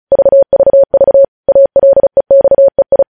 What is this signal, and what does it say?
Morse code, heard as a keyed tone, about 36 wpm, 550 Hz. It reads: VVV ALEXEI
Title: News Headlines at 35